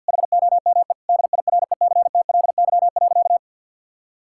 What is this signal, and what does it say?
Morse code, heard as a keyed tone, about 52 wpm, 700 Hz. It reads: HOME DIRECTLY.
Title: Sentences from Top 1000 Words - Review Part 5 - 50wpm